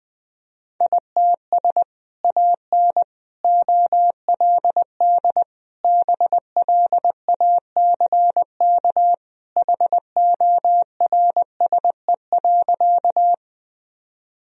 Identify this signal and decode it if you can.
Morse code, heard as a keyed tone, about 20 wpm, 700 Hz. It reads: ITS AN OLD BLACK HORSE.
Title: Sentences from Top 1000 Words - Review Part 2 - 20wpm